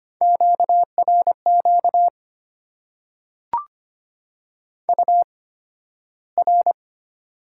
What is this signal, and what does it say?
Morse code, heard as a keyed tone, about 25 wpm, spaced as for 13 wpm, 700 Hz. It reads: QRQ UR
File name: Top 90 QSO Elements - 8x Character Spacing - 25wpm